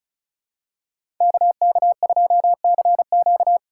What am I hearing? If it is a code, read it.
Morse code, heard as a keyed tone, about 36 wpm, 700 Hz. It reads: KK2CQ
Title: USA Call Signs - 5 Characters - 4x Character Spacing - 35wpm